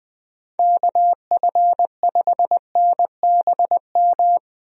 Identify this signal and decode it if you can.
Morse code, heard as a keyed tone, about 21 wpm, 700 Hz. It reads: KF5NBM
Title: USA Call Signs - 6 Characters - Repeated 2x - 20wpm